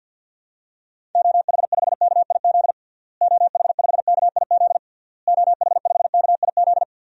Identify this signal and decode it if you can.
Morse code, heard as a keyed tone, about 50 wpm, 700 Hz. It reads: OH5KIZ OH5KIZ OH5KIZ
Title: DXCC-100 Call Signs - 6 Characters - Repeated 3x - 50wpm